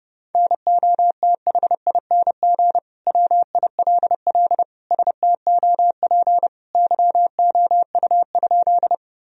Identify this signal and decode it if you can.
Morse code, heard as a keyed tone, about 32 wpm, 700 Hz. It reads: NOTHING WILL STOP YOU?
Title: Sentences from Top 1000 Words - Review Part 1 - 30wpm